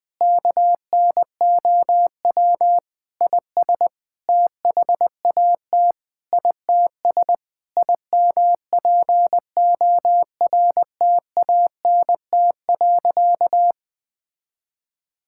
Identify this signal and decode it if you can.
Morse code, heard as a keyed tone, about 20 wpm, 700 Hz. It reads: KNOW IS THAT ITS IMPORTANT.
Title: Sentences with 6-Word Phrases - 20wpm